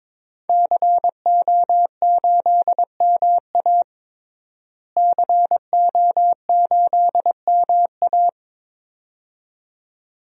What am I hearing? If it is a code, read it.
Morse code, heard as a keyed tone, about 22 wpm, 700 Hz. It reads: CO8MA CO8MA
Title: DXCC-100 Call Signs - 5 Characters - Repeated 2x - 22wpm